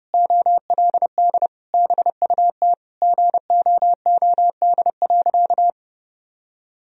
Morse code, heard as a keyed tone, about 32 wpm, 700 Hz. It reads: OLD BUT GOOD.
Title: Sentences from Top 1000 Words - Review Part 2 - 30wpm